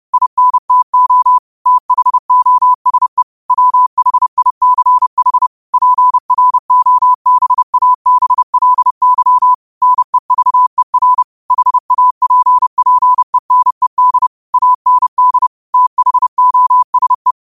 Morse code, heard as a keyed tone, about 30 wpm, 1000 Hz. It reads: INTO THOSE WHICH PROBABLY NEVER HAPPENED AND THOSE
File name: qotdfiles / 30wpm